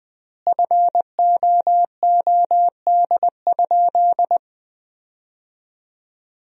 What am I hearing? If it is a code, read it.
Morse code, heard as a keyed tone, about 20 wpm, 700 Hz. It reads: FOOD?
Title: 3x Word-Spacing - Sentences from Top 300 Words - 20wpm